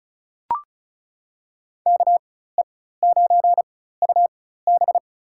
Morse code, heard as a keyed tone, about 34 wpm, spaced as for 16 wpm, 700 Hz. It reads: KE9UB
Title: USA Call Signs - 5 Characters - 4x Character Spacing - 35wpm